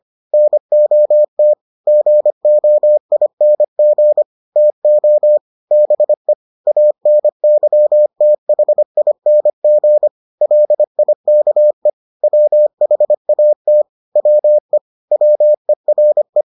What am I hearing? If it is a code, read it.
Morse code, heard as a keyed tone, about 25 wpm, 600 Hz. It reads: NOT GOING TO BE ANYTHING LIKE WHAT WE WERE